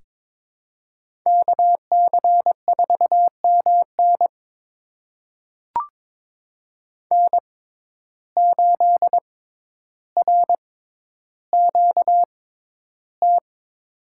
Morse code, heard as a keyed tone, about 21 wpm, spaced as for 11 wpm, 700 Hz. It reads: KC4MN N8RQT
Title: USA Call Signs - 5 Characters - 6x Character Spacing - 22wpm